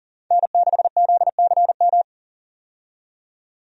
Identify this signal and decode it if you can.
Morse code, heard as a keyed tone, about 40 wpm, 700 Hz. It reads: N6ZCM
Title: USA Call Signs - 5 Characters - 40wpm